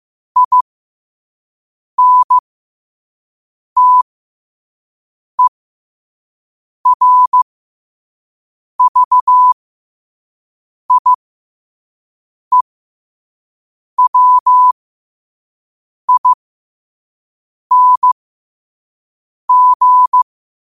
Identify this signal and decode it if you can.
Morse code, heard as a keyed tone, about 15 wpm, spaced as for 5 wpm, 1000 Hz. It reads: INTERVIEWING
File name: qotdfiles / 5wpm